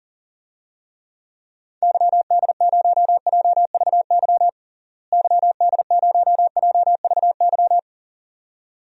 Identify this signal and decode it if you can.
Morse code, heard as a keyed tone, about 40 wpm, 700 Hz. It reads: YD0JVY YD0JVY
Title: DXCC-100 Call Signs - 6 Characters - Repeated 2x - 40wpm